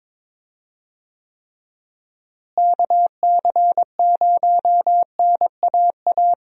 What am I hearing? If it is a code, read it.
Morse code, heard as a keyed tone, about 22 wpm, 700 Hz. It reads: KC0NAA
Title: USA Call Signs - 6 Characters - Repeated 3x - 22wpm